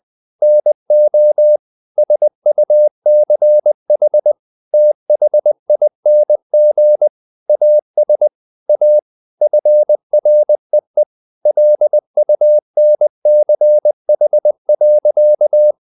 Morse code, heard as a keyed tone, about 20 wpm, 600 Hz. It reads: NO SUCH THING AS A FREE LUNCH.
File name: CW / Practice / EnglishSayings20